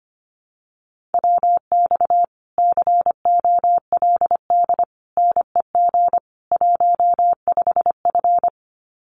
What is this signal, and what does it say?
Morse code, heard as a keyed tone, about 25 wpm, 700 Hz. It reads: WX COLD NEG 15F
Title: Common QSO Phrases Encore 25wpm